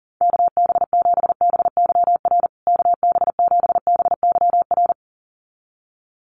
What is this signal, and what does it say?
Morse code, heard as a keyed tone, about 40 wpm, 700 Hz. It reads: KB7BYR KB7BYR
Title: USA Call Signs Repeated 2x 40wpm